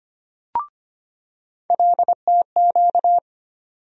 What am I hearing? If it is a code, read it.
Morse code, heard as a keyed tone, about 25 wpm, 700 Hz. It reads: LTQ
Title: USA Call Sign Postfixes - 3 Letters 25wpm